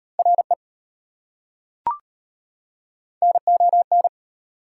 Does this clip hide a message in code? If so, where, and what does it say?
Yes: Morse code from 0.2 to 4.1 s, heard as a keyed tone, about 33 wpm, 700 Hz. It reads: RE NON